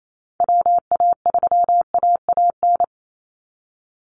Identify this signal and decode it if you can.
Morse code, heard as a keyed tone, about 28 wpm, 700 Hz. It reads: WA3AAN
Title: Call Signs Repeated 2x - 28wpm